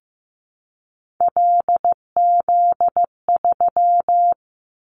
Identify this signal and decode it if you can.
Morse code, heard as a keyed tone, about 15 wpm, 700 Hz. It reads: LZ3
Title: Callsign Prefixes for Top 100 DXCC - 15wpm